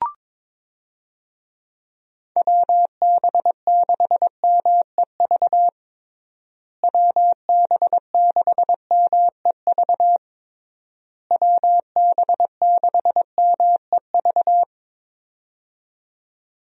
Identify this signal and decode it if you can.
Morse code, heard as a keyed tone, about 22 wpm, 700 Hz. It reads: WB6MEV WB6MEV WB6MEV
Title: USA Call Signs - 6 Characters - Repeated 3x - 22wpm